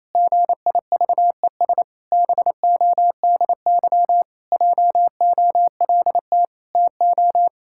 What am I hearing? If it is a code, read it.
Morse code, heard as a keyed tone, about 28 wpm, 700 Hz. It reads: GIVES BODY JOLT TO
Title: Pangram Challenge v2 - 28wpm